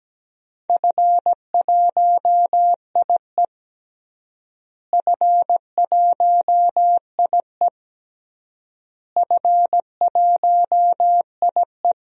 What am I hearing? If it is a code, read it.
Morse code, heard as a keyed tone, about 17 wpm, 700 Hz. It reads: F1IE F1IE F1IE
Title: DXCC-100 Call Signs - 4 Characters - Repeated 3x - 17wpm